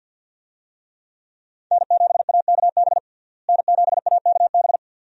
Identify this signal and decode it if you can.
Morse code, heard as a keyed tone, about 46 wpm, 700 Hz. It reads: N7AKB N7AKB
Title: USA Call Signs - 5 Characters - Repeated 2x - 50wpm